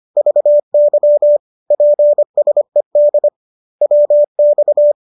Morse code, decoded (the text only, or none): VY PSED WX